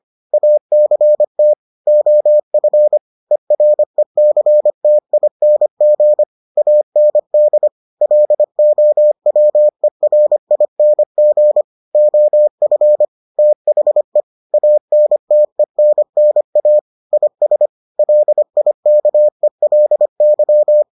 Morse code, heard as a keyed tone, about 25 wpm, 600 Hz. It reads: ACT OF ERECTING AND LOWERING OF THE ANTENNA IS LIKELY